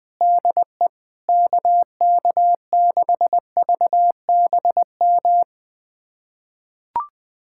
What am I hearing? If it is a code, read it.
Morse code, heard as a keyed tone, about 20 wpm, 700 Hz. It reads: DE KK6VBM
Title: USA Callsign DE Callsign - 6 Characters - 20wpm